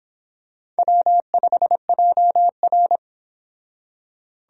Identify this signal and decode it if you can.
Morse code, heard as a keyed tone, about 26 wpm, 700 Hz. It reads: W5JR